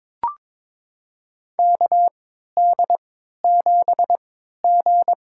Morse code, heard as a keyed tone, about 22 wpm, spaced as for 12 wpm, 700 Hz. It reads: KD7G
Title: USA Call Signs - 4 Characters - 3x Character Spacing - 22wpm